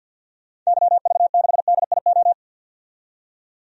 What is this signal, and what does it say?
Morse code, heard as a keyed tone, about 52 wpm, 700 Hz. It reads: YV6DIO
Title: DXCC-100 Call Signs - 6 Characters - 50wpm